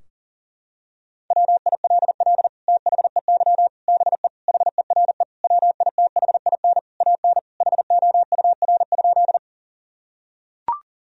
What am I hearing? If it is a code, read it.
Morse code, heard as a keyed tone, about 37 wpm, 700 Hz. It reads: WILL THEY BE HERE WITHIN AN HOUR?
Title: Sentences from Top 1000 Words - Review Part 4 - 40wpm